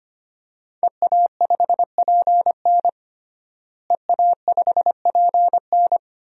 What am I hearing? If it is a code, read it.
Morse code, heard as a keyed tone, about 25 wpm, 700 Hz. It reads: EA5PN EA5PN
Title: DXCC-100 Call Signs - 5 Characters - Repeated 2x - 25wpm